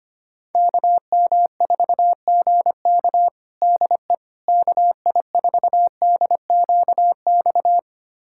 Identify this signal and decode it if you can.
Morse code, heard as a keyed tone, about 25 wpm, 700 Hz. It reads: KM4GK DE KI4DQX
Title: USA Callsign DE Callsign 25wpm